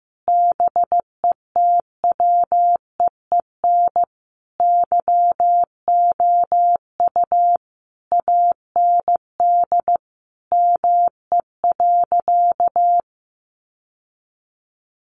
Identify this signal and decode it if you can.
Morse code, heard as a keyed tone, about 14 wpm, 700 Hz. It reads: BETWEEN YOU AND ME.
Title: Sentences from Top 1200 Words 15wpm